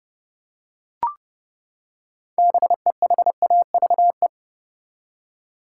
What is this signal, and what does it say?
Morse code, heard as a keyed tone, about 30 wpm, 700 Hz. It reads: BEHAVE
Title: Six Letter Words - 30wpm